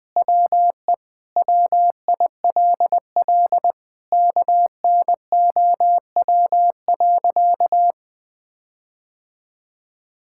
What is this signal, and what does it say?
Morse code, heard as a keyed tone, about 20 wpm, 700 Hz. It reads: WE WILL KNOW.
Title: Sentences with 4-Word Phrases - Spoken Beginning - 20wpm